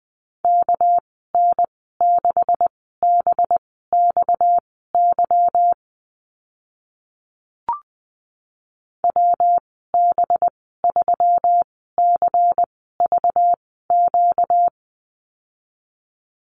Morse code, heard as a keyed tone, about 20 wpm, spaced as for 14 wpm, 700 Hz. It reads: KN6BXY WB3CVQ
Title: 245 Lesson 39g - Letter z - Call Signs 20wpm